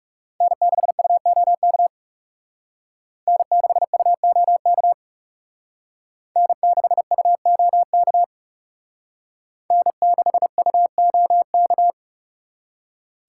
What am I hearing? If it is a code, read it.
Morse code, heard as a keyed tone, about 36 wpm, 700 Hz. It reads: N6UOK N6UOK N6UOK N6UOK
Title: Speed Racing - USA 1x3 Call Signs 30wpm